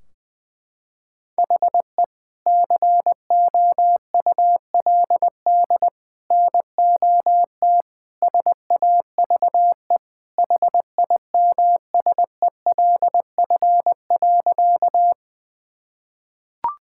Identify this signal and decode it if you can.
Morse code, heard as a keyed tone, about 20 wpm, 700 Hz. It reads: HE COULD NOT SAVE HIMSELF.
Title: Sentences from Top 1600 Words 20wpm